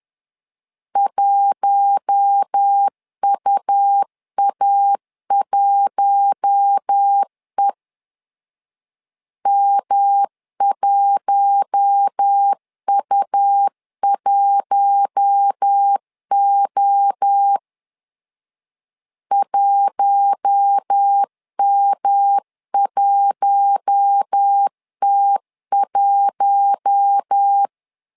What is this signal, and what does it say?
Morse code, heard as a keyed tone, about 10 wpm, 800 Hz. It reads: 1UA1E M1U1O 1M1T1
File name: LEZIONI / 4 Q1 10wpm 2448